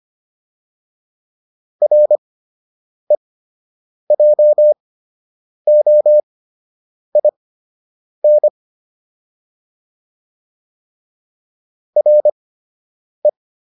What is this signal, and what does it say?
Morse code, heard as a keyed tone, about 25 wpm, spaced as for 8 wpm, 600 Hz. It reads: REJOIN RE